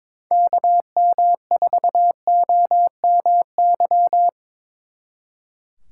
Morse code, heard as a keyed tone, about 26 wpm, 700 Hz. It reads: KM4OMY
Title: Speed Racing - USA 2x3 Call Signs 22wpm